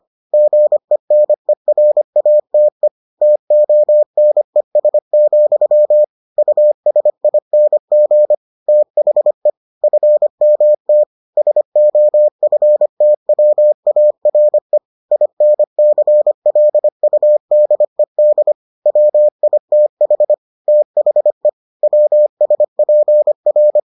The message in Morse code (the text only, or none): GENERATE TONES, USING THE FMT SOFTWARE INCLUDED WITH THE WSPR